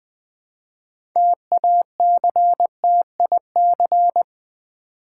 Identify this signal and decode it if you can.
Morse code, heard as a keyed tone, about 20 wpm, 700 Hz. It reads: TACTIC
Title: 6 Letter Words - First 3 Letters Given - 20wpm